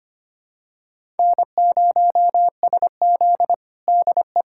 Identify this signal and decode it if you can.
Morse code, heard as a keyed tone, about 25 wpm, 700 Hz. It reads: N0SZ DE
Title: USA Callsign DE Callsign - 4 Characters - 25wpm